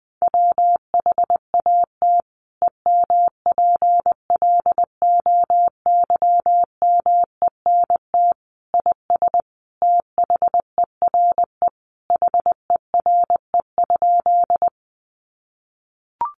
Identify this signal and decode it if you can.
Morse code, heard as a keyed tone, about 20 wpm, 700 Hz. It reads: WHAT EMPLOYMENT IS THERE HERE?
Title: Sentences from Top 1500 Words 20wpm